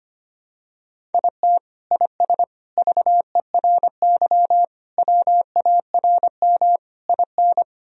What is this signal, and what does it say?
Morse code, heard as a keyed tone, about 25 wpm, 700 Hz. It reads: IT IS VERY WARM IN
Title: Sentences from Top 1000 Words - Review Part 2 - 25wpm